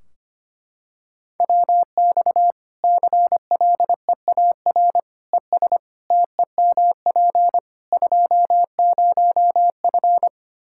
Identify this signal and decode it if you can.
Morse code, heard as a keyed tone, about 25 wpm, 700 Hz. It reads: WX CLEAR ES TEMP 20F
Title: K1USN SST QSO Practice 25wpm